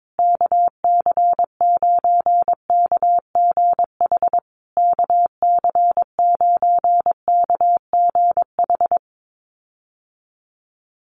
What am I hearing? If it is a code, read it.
Morse code, heard as a keyed tone, about 22 wpm, 700 Hz. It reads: KC9KGH KC9KGH
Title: USA Call Signs Repeated 3x 22wpm